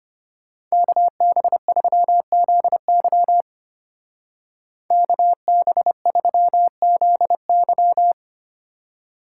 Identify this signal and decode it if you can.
Morse code, heard as a keyed tone, about 32 wpm, 700 Hz. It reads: KB3ZY KB3ZY